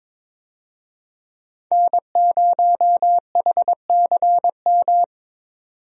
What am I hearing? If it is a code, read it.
Morse code, heard as a keyed tone, about 22 wpm, 700 Hz. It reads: N0HCM
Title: USA Call Signs - 5 Characters - 22wpm